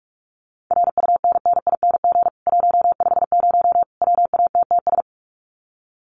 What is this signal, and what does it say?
Morse code, heard as a keyed tone, about 44 wpm, 700 Hz. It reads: RUNNING 150 WATTS
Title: Common QSO Phrases 45wpm